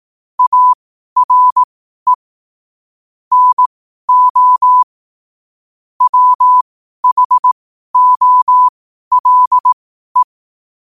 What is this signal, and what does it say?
Morse code, heard as a keyed tone, about 18 wpm, spaced as for 12 wpm, 1000 Hz. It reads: ARE NO WHOLE
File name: qotdfiles / 12wpm